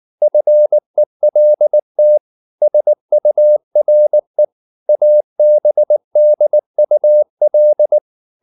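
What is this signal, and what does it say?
Morse code, heard as a keyed tone, about 19 wpm, 600 Hz. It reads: FELT SURE ABDUL
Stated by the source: Morse code at 19 wpm from CW med Gnister, reading Ken Follett's novel Never in English.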